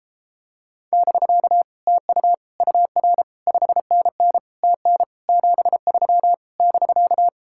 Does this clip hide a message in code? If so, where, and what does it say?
Yes: Morse code from 0.9 to 7.3 s, heard as a keyed tone, about 33 wpm, 700 Hz. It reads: <BK> TU UR 5NN TN 73 <BK>